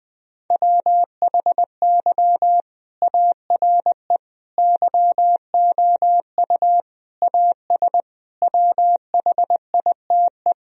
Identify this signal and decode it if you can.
Morse code, heard as a keyed tone, about 21 wpm, 700 Hz. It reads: WHY ARE YOU AS WHITE
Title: Sentences from Top 1000 Words - Review Part 3 - 20wpm